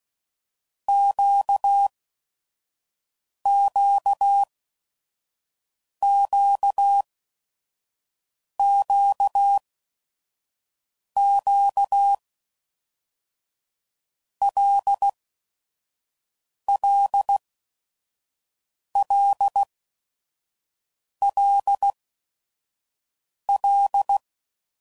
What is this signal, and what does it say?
Morse code, heard as a keyed tone, about 16 wpm, spaced as for 5 wpm, 800 Hz. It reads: QQQQQ LLLLL
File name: Basic Letter Groups By7Groups / LFCYQ / LFCYQ@5 f15